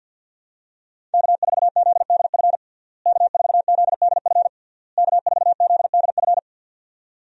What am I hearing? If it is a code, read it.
Morse code, heard as a keyed tone, about 50 wpm, 700 Hz. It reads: K4ZDF K4ZDF K4ZDF
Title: USA Call Signs - 5 Characters - Repeated 3x - 50wpm